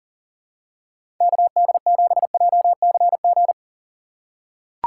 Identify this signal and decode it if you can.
Morse code, heard as a keyed tone, about 40 wpm, 700 Hz. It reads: KD7JCG
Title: USA Call Signs - 2x Character Spacing - 40wpm